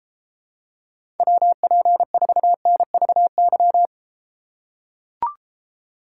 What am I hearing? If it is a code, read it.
Morse code, heard as a keyed tone, about 33 wpm, 700 Hz. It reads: WP4NVY